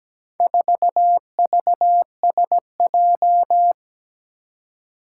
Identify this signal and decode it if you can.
Morse code, heard as a keyed tone, about 17 wpm, 700 Hz. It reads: 4VSJ
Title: Four Challenging Letter-Number Combinations 17wpm